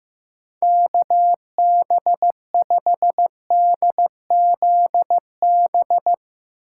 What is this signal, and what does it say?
Morse code, heard as a keyed tone, about 17 wpm, 700 Hz. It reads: KB5DZB
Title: Speed Racing - USA 2x3 Call Signs 15wpm